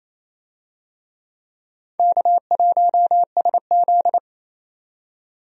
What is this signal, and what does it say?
Morse code, heard as a keyed tone, about 28 wpm, 700 Hz. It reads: K1SZ
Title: USA Call Signs - 4 Characters - 28wpm